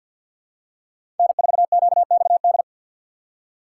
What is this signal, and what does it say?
Morse code, heard as a keyed tone, about 51 wpm, 700 Hz. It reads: N4QXD
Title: USA Call Signs - 5 Characters - 2x Character Spacing - 50wpm